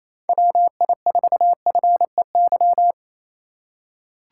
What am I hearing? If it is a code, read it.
Morse code, heard as a keyed tone, about 28 wpm, 700 Hz. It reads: WI4FEY
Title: USA Call Signs - 6 Characters - 4x Character Spacing - 28wpm